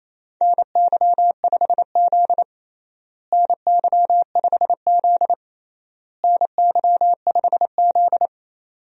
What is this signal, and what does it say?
Morse code, heard as a keyed tone, about 28 wpm, 700 Hz. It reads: NY5Z NY5Z NY5Z
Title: USA Call Signs - 4 Characters - Repeated 3x - 28wpm